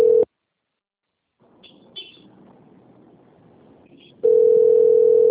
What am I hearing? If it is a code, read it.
Ringback tone (North America), 440 + 480 Hz.